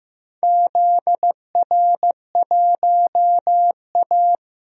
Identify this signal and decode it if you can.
Morse code, heard as a keyed tone, about 15 wpm, 700 Hz. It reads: ZR1A
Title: DXCC-100 Call Signs - 4 Characters - 15wpm